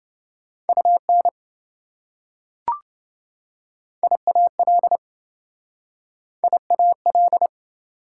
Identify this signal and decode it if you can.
Morse code, heard as a keyed tone, about 26 wpm, 700 Hz. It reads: UN IAL IAL